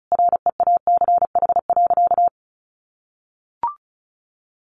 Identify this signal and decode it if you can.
Morse code, heard as a keyed tone, about 34 wpm, 700 Hz. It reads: REACH.
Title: Sentences from Top 1400 Words 35wpm